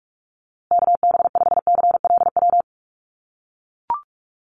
Morse code, heard as a keyed tone, about 44 wpm, 700 Hz. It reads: KB5CLW
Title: USA Call Signs Repeated 2x 45wpm